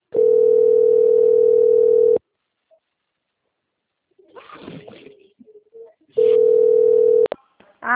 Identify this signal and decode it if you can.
Ringback tone (North America), 440 + 480 Hz.